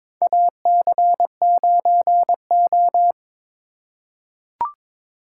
Morse code, heard as a keyed tone, about 22 wpm, 700 Hz. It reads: AC9O
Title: USA Call Signs - 4 Characters - 3x Character Spacing - 22wpm